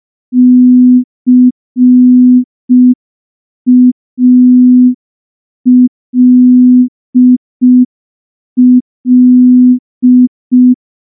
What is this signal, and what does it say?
Morse code, heard as a keyed tone, about 5 wpm, 250 Hz. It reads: CALL
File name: morse / morse-training-05WPM